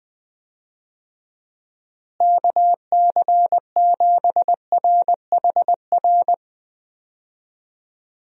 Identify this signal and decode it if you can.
Morse code, heard as a keyed tone, about 20 wpm, 700 Hz. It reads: KC7RHR
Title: USA Call Signs - 6 Characters - Repeated 3x - 20wpm